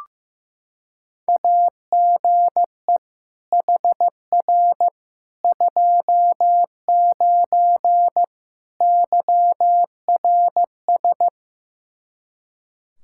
Morse code, heard as a keyed tone, about 15 wpm, 700 Hz. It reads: AGE HR 29 YRS
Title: Practice QSO - Exchange 3 - More Info - 15wpm